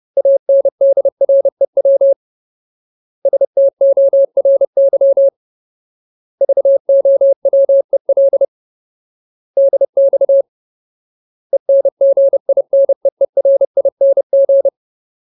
Morse code, heard as a keyed tone, about 30 wpm, 550 Hz. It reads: ANDREW STORY VOWEL DX ENGINEERING
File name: Ultimate Word List / 30WPM Ultimate